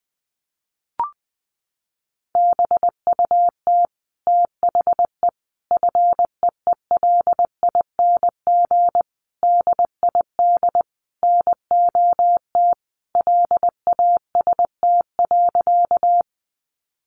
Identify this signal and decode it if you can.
Morse code, heard as a keyed tone, about 20 wpm, 700 Hz. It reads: BUT THE FEELING DID NOT LAST.
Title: Sentences from Top 1200 Words 20wpm